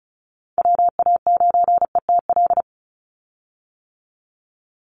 Morse code, heard as a keyed tone, about 34 wpm, 700 Hz. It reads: WA9ETL
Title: USA Call Signs 35wpm